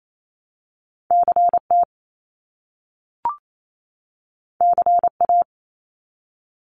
Morse code, heard as a keyed tone, about 28 wpm, 700 Hz. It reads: CT CA
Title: US States 28wpm